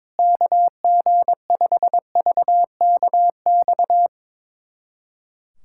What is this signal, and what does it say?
Morse code, heard as a keyed tone, about 21 wpm, 700 Hz. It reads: KG5VKX
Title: USA Callsign DE Callsign - 6 Characters - 22wpm